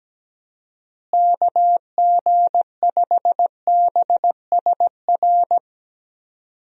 Morse code, heard as a keyed tone, about 17 wpm, 700 Hz. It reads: KG5BSR
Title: USA Call Signs - 6 Characters - Repeated 3x - 17wpm